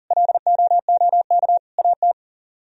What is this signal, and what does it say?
Morse code, heard as a keyed tone, about 51 wpm, 700 Hz. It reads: LOOK AT